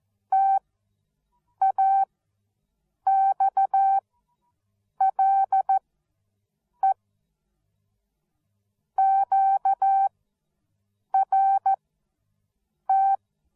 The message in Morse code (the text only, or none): TAXLE QRT